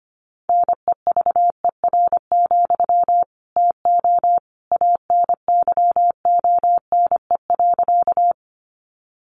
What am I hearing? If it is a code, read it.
Morse code, heard as a keyed tone, about 25 wpm, 700 Hz. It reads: NEVER, TO ANYONE.
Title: Sentences from Top 900 Words 25wpm